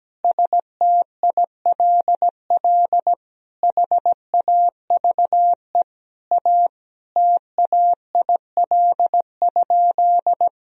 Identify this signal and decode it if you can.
Morse code, heard as a keyed tone, about 17 wpm, 700 Hz. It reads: STILL HAVE A TAIL?
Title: Sentences from Top 1000 Words - Review Part 3 - 17wpm